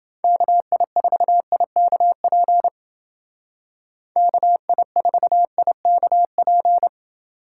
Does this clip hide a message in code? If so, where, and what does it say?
Yes: Morse code from 0.2 to 6.9 s, heard as a keyed tone, about 29 wpm, 700 Hz. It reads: KI4IKP KI4IKP